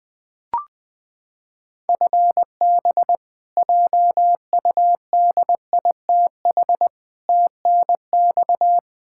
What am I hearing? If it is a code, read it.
Morse code, heard as a keyed tone, about 21 wpm, 700 Hz. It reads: FB JUDITH TNX
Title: K1USN SST QSO Practice 20wpm